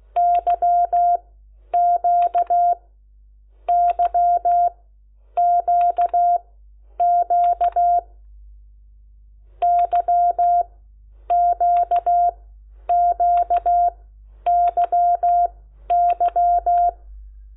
Morse code, heard as a keyed tone, about 16 wpm, spaced as for 9 wpm, 700 Hz. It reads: YQYQQ YQQYY
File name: Lecciones / CLASE Nº10